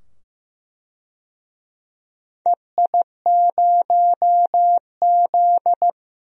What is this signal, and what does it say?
Morse code, heard as a keyed tone, about 15 wpm, 700 Hz. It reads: EI0Z
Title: DXCC-100 Call Signs - 4 Characters - 15wpm